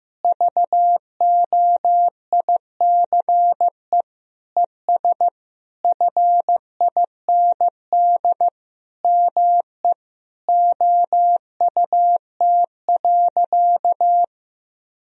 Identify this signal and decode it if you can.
Morse code, heard as a keyed tone, about 15 wpm, 700 Hz. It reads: VOICE ES FIND ME OUT.
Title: Sentences from Top 500 Words and Abbreviations 15wpm